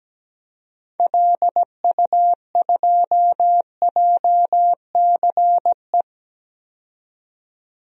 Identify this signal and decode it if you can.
Morse code, heard as a keyed tone, about 17 wpm, 700 Hz. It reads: LU2JCE
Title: DXCC-100 Call Signs - 6 Characters - 17wpm